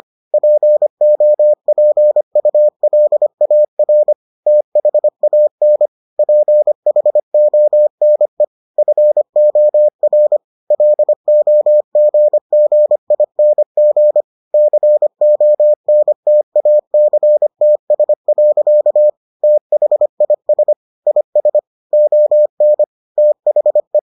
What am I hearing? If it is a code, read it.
Morse code, heard as a keyed tone, about 25 wpm, 600 Hz. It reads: POPULAR THAN PHONE FOR LOGGING CONTACTS. THIS IS ON THE